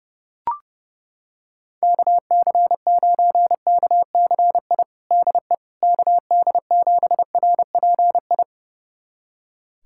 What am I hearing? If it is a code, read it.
Morse code, heard as a keyed tone, about 30 wpm, 700 Hz. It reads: KC9KCI DE KD7RPI
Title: USA Callsign DE Callsign 30wpm